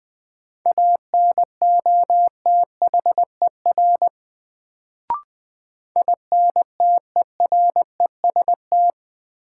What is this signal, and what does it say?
Morse code, heard as a keyed tone, about 20 wpm, 700 Hz. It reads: ANOTHER INTEREST
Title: Sets of 5 Words Encore - New Word Drill - 20wpm